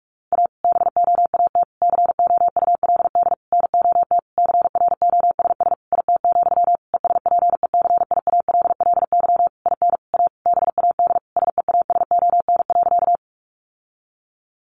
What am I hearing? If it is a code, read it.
Morse code, heard as a keyed tone, about 44 wpm, 700 Hz. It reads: A BOAT COULD NOT CROSS IT, ESPECIALLY IN A BAD SEASON.
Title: Sentences from Top 1400 Words 45wpm